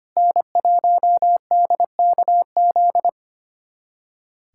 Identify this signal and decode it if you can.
Morse code, heard as a keyed tone, about 25 wpm, 700 Hz. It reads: N1DKZ
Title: USA Callsign DE Callsign - 5 Characters - 25wpm